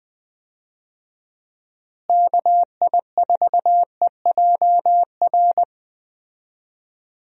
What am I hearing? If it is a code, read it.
Morse code, heard as a keyed tone, about 20 wpm, 700 Hz. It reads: KI4EJR
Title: USA Call Signs - 6 Characters - Repeated 3x - 20wpm